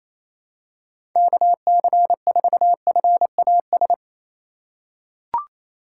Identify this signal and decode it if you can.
Morse code, heard as a keyed tone, about 28 wpm, 700 Hz. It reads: KC4FAS
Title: USA Call Signs - 4x Character Spacing - 28wpm